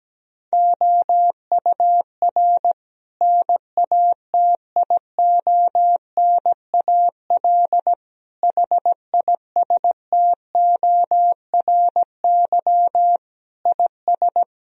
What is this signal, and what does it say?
Morse code, heard as a keyed tone, about 17 wpm, 700 Hz. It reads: OUR NATIONAL HISTORY IS
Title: Sentences from Top 1000 Words - Review Part 4 - 17wpm